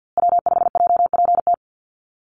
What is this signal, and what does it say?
Morse code, heard as a keyed tone, about 49 wpm, 700 Hz. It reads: W5JPT
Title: USA Call Signs Repeated 2x 50wpm